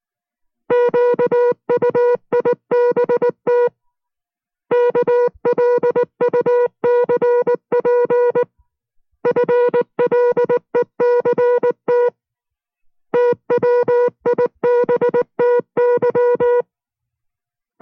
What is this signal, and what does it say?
Morse code, heard as a keyed tone, about 19 wpm, 480 Hz. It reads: QUIBT KLUCP FLECT TWIBTY